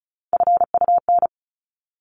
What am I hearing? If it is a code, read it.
Morse code, heard as a keyed tone, about 34 wpm, 700 Hz. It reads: FUN
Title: Top 501-750 Words 35wpm